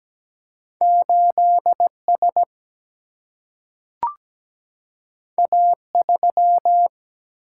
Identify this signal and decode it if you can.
Morse code, heard as a keyed tone, about 17 wpm, 700 Hz. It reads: 8 S A 3
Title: Single-Letter Single-Number 17wpm